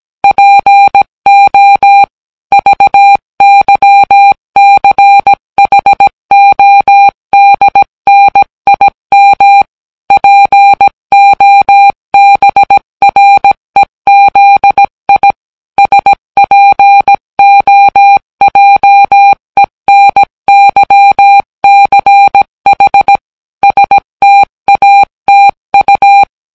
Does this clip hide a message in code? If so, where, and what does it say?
Yes: Morse code from 0.2 to 26.2 s, heard as a keyed tone, about 17 wpm, 800 Hz. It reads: PO VYCHODNIM POBREZI SPOJENYCH STATU